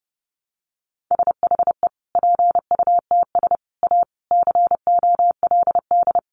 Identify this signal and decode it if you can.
Morse code, heard as a keyed tone, about 30 wpm, 700 Hz. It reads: SHE PUTS A COLD
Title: Easy Phrases 30wpm